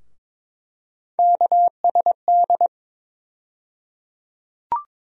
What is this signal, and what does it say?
Morse code, heard as a keyed tone, about 22 wpm, 700 Hz. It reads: KSD
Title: USA Call Sign Postfixes - 3 Letters 22wpm